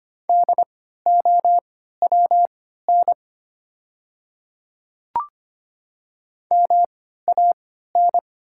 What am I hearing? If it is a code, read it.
Morse code, heard as a keyed tone, about 25 wpm, spaced as for 14 wpm, 700 Hz. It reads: DOWN MAN
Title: 3x Character Spacing - Top 300 Words - 25wpm